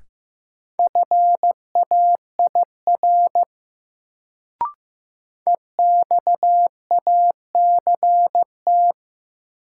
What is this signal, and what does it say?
Morse code, heard as a keyed tone, about 15 wpm, 700 Hz. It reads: FAIR EXACT
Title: Sentences from Top 1300 Words - New Word Drill Alt - 15wpm